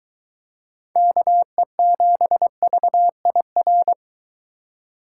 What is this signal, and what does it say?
Morse code, heard as a keyed tone, about 22 wpm, 700 Hz. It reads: KE7VIR